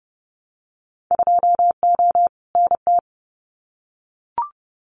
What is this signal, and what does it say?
Morse code, heard as a keyed tone, about 30 wpm, 700 Hz. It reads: 2O NT
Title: Winter Field Day Exchanges Encore 30wpm